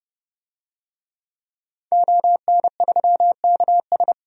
Morse code, heard as a keyed tone, about 30 wpm, 700 Hz. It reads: ON3KS
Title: DXCC-100 Call Signs - 5 Characters - 30wpm